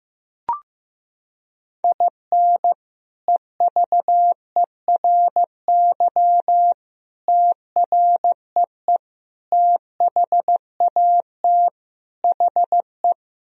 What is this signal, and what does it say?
Morse code, heard as a keyed tone, about 15 wpm, 700 Hz. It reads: IN EVERY TREE THAT HE
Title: Sentences from Top 1000 Words - Review Part 1 - 15wpm